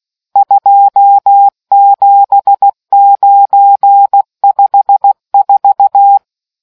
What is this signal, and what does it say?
Morse code, heard as a keyed tone, about 16 wpm, 800 Hz. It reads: 27954